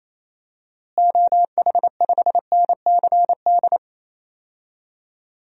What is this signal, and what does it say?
Morse code, heard as a keyed tone, about 27 wpm, 700 Hz. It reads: OH5NCD
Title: DXCC-100 Call Signs - 6 Characters - Repeated 3x - 28wpm